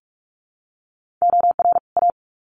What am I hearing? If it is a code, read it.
Morse code, heard as a keyed tone, about 44 wpm, 700 Hz. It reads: OR A
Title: Sets of 2 Words 45wpm